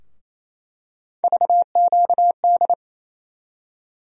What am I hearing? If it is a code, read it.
Morse code, heard as a keyed tone, about 28 wpm, 700 Hz. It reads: VQD
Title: USA Call Sign Postfixes - 3 Letters 28wpm